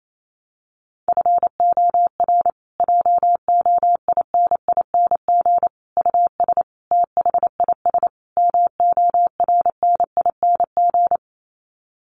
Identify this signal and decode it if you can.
Morse code, heard as a keyed tone, about 28 wpm, 700 Hz. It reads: FOR JOINING US THIS MORNING
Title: Sets of 5 Words Encore 28wpm